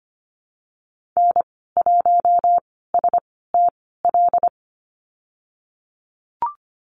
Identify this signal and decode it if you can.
Morse code, heard as a keyed tone, about 25 wpm, spaced as for 16 wpm, 700 Hz. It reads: N1STL
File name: Lesson 15g - Number 5 - Call Signs 25wpm